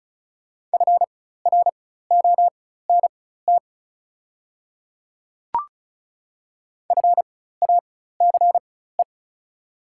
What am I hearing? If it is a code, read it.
Morse code, heard as a keyed tone, about 36 wpm, spaced as for 16 wpm, 700 Hz. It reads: FRONT FACE
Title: 4x Character Spacing - Top 500 Words - 35wpm